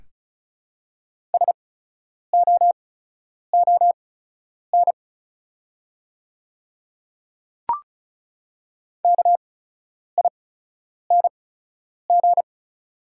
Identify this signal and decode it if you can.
Morse code, heard as a keyed tone, about 36 wpm, spaced as for 10 wpm, 700 Hz. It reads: SOON KING
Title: 8x Character Spacing - Top 500 Words - 35wpm